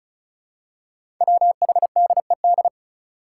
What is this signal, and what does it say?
Morse code, heard as a keyed tone, about 36 wpm, 700 Hz. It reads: WHDED
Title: Five Challenging Letter-Number Combinations 35wpm